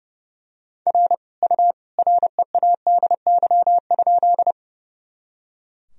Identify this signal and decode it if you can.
Morse code, heard as a keyed tone, about 30 wpm, 700 Hz. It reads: R U READY?
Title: Sentences from Top 400 Words and Abbreviations 30wpm